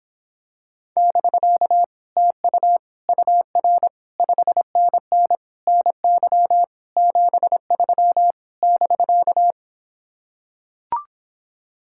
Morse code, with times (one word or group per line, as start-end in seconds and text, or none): 1.0-1.8 <BK>
2.2-2.8 TU
3.1-3.9 UR
4.2-5.4 5NN
5.7-6.6 NY
7.0-8.3 73
8.6-9.5 <BK>